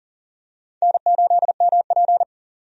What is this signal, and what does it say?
Morse code, heard as a keyed tone, about 40 wpm, 700 Hz. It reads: N8MP
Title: USA Call Signs - 4 Characters - 40wpm